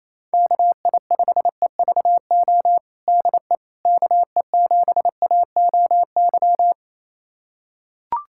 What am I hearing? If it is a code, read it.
Morse code, heard as a keyed tone, about 28 wpm, 700 Hz. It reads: KI5EVO DE KE7AOY
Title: USA Callsign DE Callsign 28wpm